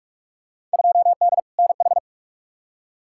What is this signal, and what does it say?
Morse code, heard as a keyed tone, about 46 wpm, 700 Hz. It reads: 2D NH
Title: Field Day Exchanges Encore - 45wpm